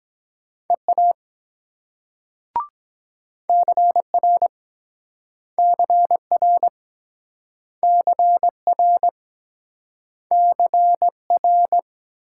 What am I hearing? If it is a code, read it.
Morse code, heard as a keyed tone, about 24 wpm, 700 Hz. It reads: EA CR CR CR CR